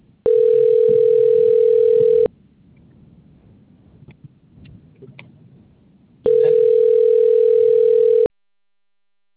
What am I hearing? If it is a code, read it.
Ringback tone (North America), 440 + 480 Hz.